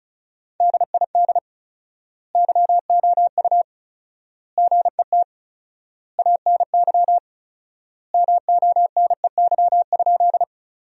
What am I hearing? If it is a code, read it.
Morse code, heard as a keyed tone, about 36 wpm, 700 Hz. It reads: DID YOU GET ANY MONEY?
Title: 4x Word-Spacing - Sentences from Top 400 Words - 35wpm